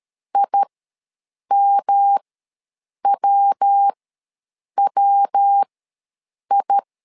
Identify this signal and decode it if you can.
Morse code, heard as a keyed tone, about 13 wpm, spaced as for 7 wpm, 800 Hz. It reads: IMWWI
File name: LEZIONI / 3 MWRO / 3 MW ws 12wpm 1816